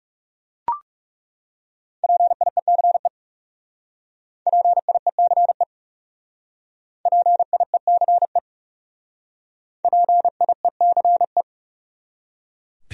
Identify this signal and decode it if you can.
Morse code, heard as a keyed tone, about 35 wpm, 700 Hz. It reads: PIECE PIECE PIECE PIECE
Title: Speed-Racing Top 500 Words 30wpm